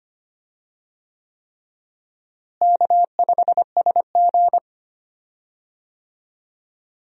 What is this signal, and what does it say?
Morse code, heard as a keyed tone, about 25 wpm, 700 Hz. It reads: K5SG
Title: USA Call Signs - 4 Characters - Repeated 3x - 25wpm